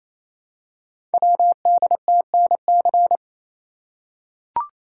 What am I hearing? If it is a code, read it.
Morse code, heard as a keyed tone, about 27 wpm, 700 Hz. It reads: WDTNC
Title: Code Groups - 5 Letters - 28wpm